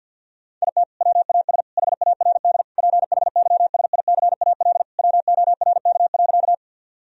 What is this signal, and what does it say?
Morse code, heard as a keyed tone, about 50 wpm, 700 Hz. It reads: IT WAS HARD PHYSICAL WORK.
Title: Sentences from Top 1000 Words - Review Part 4 - 50wpm